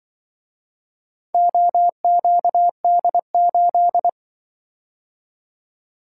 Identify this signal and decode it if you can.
Morse code, heard as a keyed tone, about 24 wpm, 700 Hz. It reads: OQD8
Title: Speed-Racing - Four Letter-Number Combinations - 20wpm